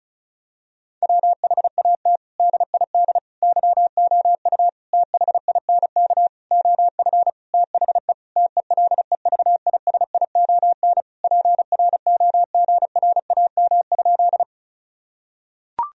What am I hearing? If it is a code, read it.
Morse code, heard as a keyed tone, about 36 wpm, 700 Hz. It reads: WHAT DID YOU THINK OF THE TELEVISION PROGRAM?
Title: Sentences from Top 1000 Words - Review Part 4 - 35wpm